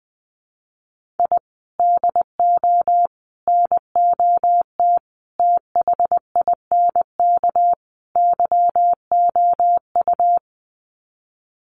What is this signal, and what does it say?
Morse code, heard as a keyed tone, about 20 wpm, 700 Hz. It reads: I DO NOT THINK YOU
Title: Sets of 5 Words 20wpm